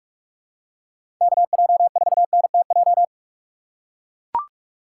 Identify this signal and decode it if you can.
Morse code, heard as a keyed tone, about 41 wpm, 700 Hz. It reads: KJ4NTJ
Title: USA Call Signs - 6 Characters - 2x Character Spacing - 45wpm